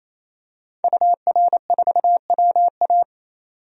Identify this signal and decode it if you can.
Morse code, heard as a keyed tone, about 28 wpm, 700 Hz. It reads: UR4WA
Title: DXCC-100 Call Signs - 5 Characters - 28wpm